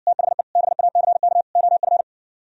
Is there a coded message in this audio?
Yes: Morse code from 0.1 to 2.0 s, heard as a keyed tone, about 60 wpm, 700 Hz. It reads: THE BACK OF